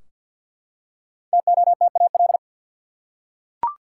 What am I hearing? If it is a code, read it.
Morse code, heard as a keyed tone, about 50 wpm, 700 Hz. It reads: TOTAL
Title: Five Letter Words - 50wpm